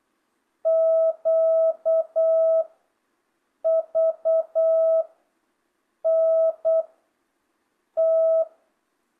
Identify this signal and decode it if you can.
Morse code, heard as a keyed tone, about 8 wpm, 660 Hz. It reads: Q V N T
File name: jh8cbh / 2025cw